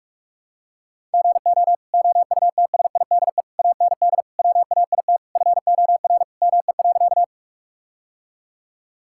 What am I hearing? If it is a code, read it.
Morse code, heard as a keyed tone, about 46 wpm, 700 Hz. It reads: GO OUTSIDE AND WAIT FOR ME.
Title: Sentences from Top 1000 Words - Review Part 6 - 45wpm